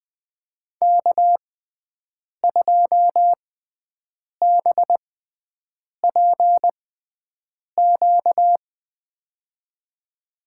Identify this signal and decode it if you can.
Morse code, heard as a keyed tone, about 20 wpm, spaced as for 7 wpm, 700 Hz. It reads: K2BPQ
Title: USA Call Signs - 5 Characters - 6x Character Spacing - 20wpm